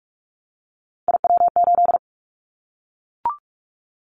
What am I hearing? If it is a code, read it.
Morse code, heard as a keyed tone, about 44 wpm, 700 Hz. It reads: IW8
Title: Callsign Prefixes for Top 25 DXCC - Repeated 3x - 45wpm